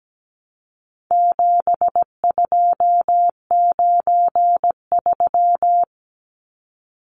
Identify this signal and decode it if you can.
Morse code, heard as a keyed tone, about 17 wpm, 700 Hz. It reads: 7293
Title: Numbers - 4-Digit 17wpm